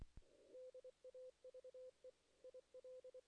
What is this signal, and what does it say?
Morse code, heard as a keyed tone, about 24 wpm, 510 Hz. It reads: DAVE IL
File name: 2016 JAN CW NAQP / 2016 Jan naqp k5ot - 0-7